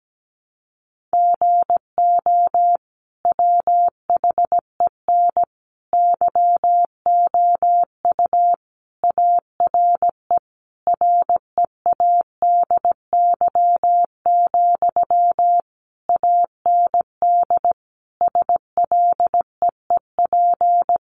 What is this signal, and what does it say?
Morse code, heard as a keyed tone, about 17 wpm, 700 Hz. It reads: GO WHEN YOU ARE READY, AND SLEEP
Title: Sentences from Top 500 Words 17wpm